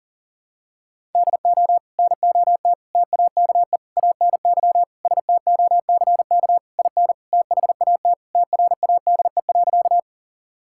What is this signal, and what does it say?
Morse code, heard as a keyed tone, about 40 wpm, 700 Hz. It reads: DO NOT TAKE ANY STOCK IN THAT TRADE.
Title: Sentences from Top 1000 Words - Review Part 5 - 40wpm